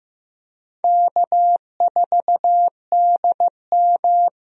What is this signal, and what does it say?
Morse code, heard as a keyed tone, about 16 wpm, 700 Hz. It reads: K4DM
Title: USA Call Signs - 4 Characters - Repeated 2x - 15wpm